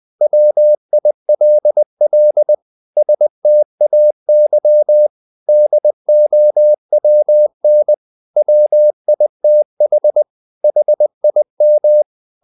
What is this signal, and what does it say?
Morse code, heard as a keyed tone, about 20 wpm, 600 Hz. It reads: WILL STAY DOWN WITH HIM
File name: the old man and the sea / the old man and the sea part 1 20wpm